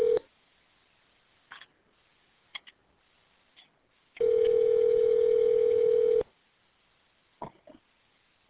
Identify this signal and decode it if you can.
Ringback tone (North America), 440 + 480 Hz.